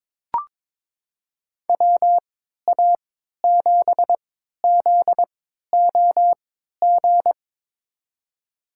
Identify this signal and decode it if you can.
Morse code, heard as a keyed tone, about 22 wpm, spaced as for 13 wpm, 700 Hz. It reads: WA7ZOG
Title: USA Call Signs - 3x Character Spacing - 22wpm